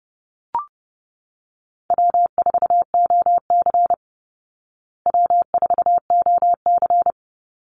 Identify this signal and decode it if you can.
Morse code, heard as a keyed tone, about 30 wpm, 700 Hz. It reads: W4OC W4OC
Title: Easy Call Signs Repeated 3x - 30wpm